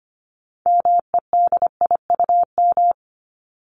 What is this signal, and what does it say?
Morse code, heard as a keyed tone, about 25 wpm, 700 Hz. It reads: MEDIUM
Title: Top 751-1000 Words 25wpm